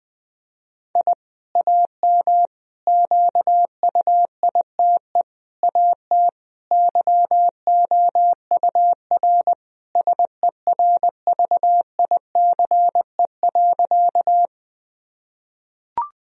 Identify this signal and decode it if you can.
Morse code, heard as a keyed tone, about 20 wpm, 700 Hz. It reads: I AM QUITE AT YOUR SERVICE.
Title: Sentences from Top 1000 Words - Review Part 4 - 20wpm